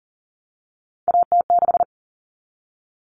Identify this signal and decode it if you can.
Morse code, heard as a keyed tone, about 40 wpm, 700 Hz. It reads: AT6
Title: Common Cut Numbers - 3-Digit 40wpm